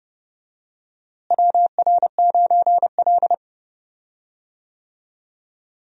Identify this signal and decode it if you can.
Morse code, heard as a keyed tone, about 29 wpm, 700 Hz. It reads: WR9L
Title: USA Call Signs - 4 Characters - Repeated 3x - 30wpm